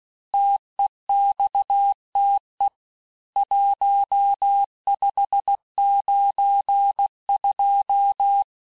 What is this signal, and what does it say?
Morse code, heard as a keyed tone, about 16 wpm, 800 Hz. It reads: TEXTE 1592
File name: cours-du-lundi files / cours-1592--teste-chiffre-vitesse-15mpm